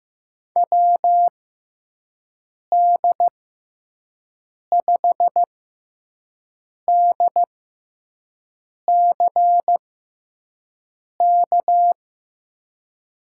Morse code, heard as a keyed tone, about 15 wpm, spaced as for 5 wpm, 700 Hz. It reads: WD5DCK
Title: USA Call Signs - 6 Characters - 6x Character Spacing - 15wpm